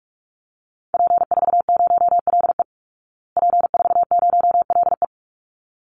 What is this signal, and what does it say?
Morse code, heard as a keyed tone, about 44 wpm, 700 Hz. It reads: P40LE P40LE
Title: Call Signs Repeated 3x - 45wpm